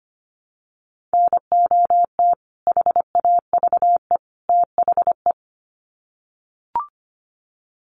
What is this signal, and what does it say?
Morse code, heard as a keyed tone, about 25 wpm, 700 Hz. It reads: NOT HAVE THE
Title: Sets of 3 Words 25wpm